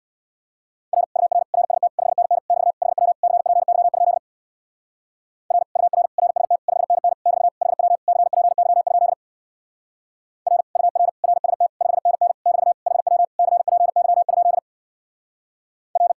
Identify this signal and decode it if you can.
Morse code, heard as a keyed tone, about 58 wpm, 700 Hz. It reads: R UR RST 5NN <BT> HW CPY? R UR RST 5NN <BT> HW CPY? R UR RST 5NN <BT> HW CPY? R
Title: Speed-Racing Common QSO Phrases - 50wpm